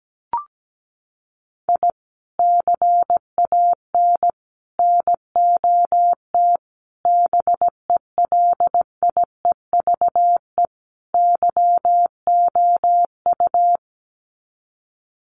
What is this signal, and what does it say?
Morse code, heard as a keyed tone, about 17 wpm, 700 Hz. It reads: I CAN NOT BELIEVE YOU
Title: Sets of 5 Words Encore 17wpm